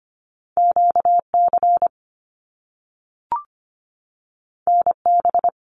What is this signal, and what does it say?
Morse code, heard as a keyed tone, about 25 wpm, 700 Hz. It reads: QC NB
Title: Canadian Provinces and Territories 25wpm